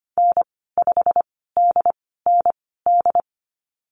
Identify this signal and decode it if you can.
Morse code, heard as a keyed tone, about 25 wpm, spaced as for 16 wpm, 700 Hz. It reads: N5DND
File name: Lesson 15g - Number 5 - Call Signs 25wpm